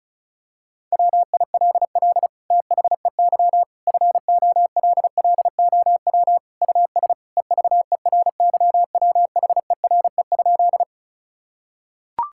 Morse code, heard as a keyed tone, about 36 wpm, 700 Hz. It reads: WILL THEY FOLLOW US EVERYWHERE?
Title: Sentences from Top 1600 Words 35wpm